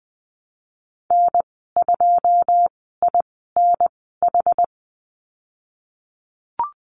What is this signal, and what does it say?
Morse code, heard as a keyed tone, about 20 wpm, spaced as for 15 wpm, 700 Hz. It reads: N2INH
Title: 173 Lesson 28g - Number 9 - Call Signs 20wpm